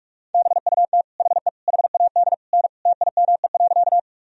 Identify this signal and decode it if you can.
Morse code, heard as a keyed tone, about 45 wpm, 700 Hz. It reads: BUT HE HAD N TIME.
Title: Sentences from Top 100 Words and Abbreviations 45wpm